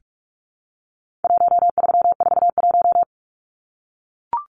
Morse code, heard as a keyed tone, about 44 wpm, 700 Hz. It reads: 1341
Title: Numbers - 4-Digit 45wpm